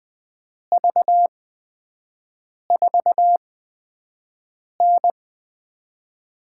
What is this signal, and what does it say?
Morse code, heard as a keyed tone, about 20 wpm, spaced as for 5 wpm, 700 Hz. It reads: V4N
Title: USA Call Signs - 5 Characters - 8x Character Spacing - 20wpm